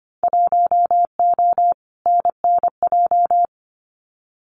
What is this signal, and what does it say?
Morse code, heard as a keyed tone, about 25 wpm, 700 Hz. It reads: 1O NNJ
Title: Winter Field Day Exchanges 25wpm